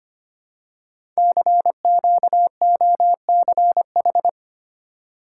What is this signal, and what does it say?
Morse code, heard as a keyed tone, about 25 wpm, 700 Hz. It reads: CQOCH
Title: Code Groups - 5 Letter:Numbers - 25wpm